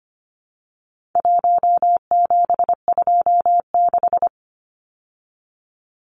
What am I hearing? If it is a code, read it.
Morse code, heard as a keyed tone, about 25 wpm, 700 Hz. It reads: 1726
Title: Numbers - 4-Digit 25wpm